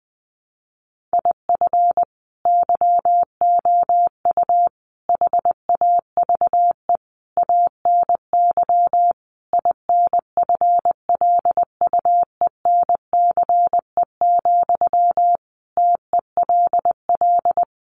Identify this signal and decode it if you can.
Morse code, heard as a keyed tone, about 20 wpm, 700 Hz. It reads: IF YOU HAVE ANY INFLUENCE, TELL
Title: Sentences from Top 1100 Words 20wpm